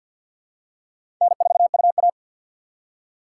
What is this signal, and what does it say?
Morse code, heard as a keyed tone, about 48 wpm, 700 Hz. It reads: N4UA
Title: USA Call Signs - 4 Characters - 2x Character Spacing - 50wpm